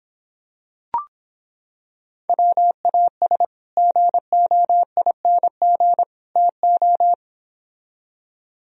Morse code, heard as a keyed tone, about 26 wpm, 700 Hz. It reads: WAS GOING TO